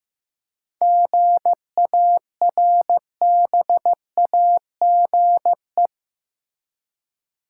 Morse code, heard as a keyed tone, about 15 wpm, 700 Hz. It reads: GARBAGE
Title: Seven Letter Words - 15wpm